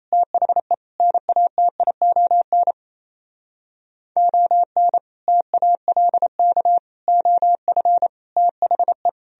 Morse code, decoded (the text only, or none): THE NATION ON TALK OF THE